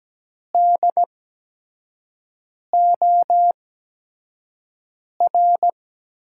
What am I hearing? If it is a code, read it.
Morse code, heard as a keyed tone, about 17 wpm, spaced as for 5 wpm, 700 Hz. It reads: DOR
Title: USA Call Signs - 5 Characters - 8x Character Spacing - 17wpm